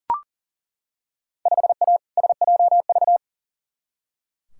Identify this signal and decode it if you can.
Morse code, heard as a keyed tone, about 40 wpm, 700 Hz. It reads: 5A SJV
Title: Field Day Exchanges - 40wpm